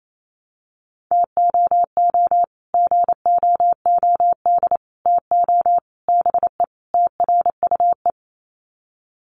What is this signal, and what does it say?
Morse code, heard as a keyed tone, about 28 wpm, 700 Hz. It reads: TOO GOOD TO BE TRUE
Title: Sets of 5 Words Encore 28wpm